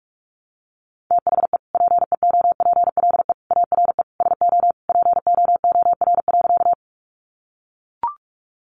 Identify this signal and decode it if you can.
Morse code, heard as a keyed tone, about 44 wpm, 700 Hz. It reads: THE PEOPLE ARE SO POOR.
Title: Sentences from Top 1000 Words 45wpm